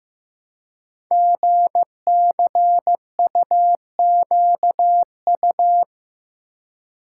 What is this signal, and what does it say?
Morse code, heard as a keyed tone, about 15 wpm, 700 Hz. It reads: GCUQU
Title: Code Groups - 5 Letters - 15wpm